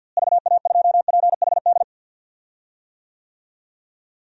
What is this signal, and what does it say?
Morse code, heard as a keyed tone, about 52 wpm, 700 Hz. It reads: VA2PHD
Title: DXCC-100 Call Signs - 6 Characters - Repeated 2x - 50wpm